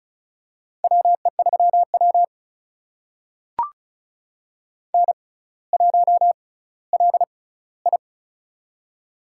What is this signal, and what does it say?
Morse code, heard as a keyed tone, about 34 wpm, spaced as for 18 wpm, 700 Hz. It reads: WE3W N1LI
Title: USA Call Signs - 4 Characters - 6x Character Spacing - 35wpm